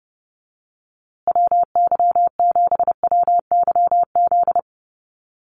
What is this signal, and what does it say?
Morse code, heard as a keyed tone, about 28 wpm, 700 Hz. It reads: WY7WYZ
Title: USA Call Signs Repeated 2x 30wpm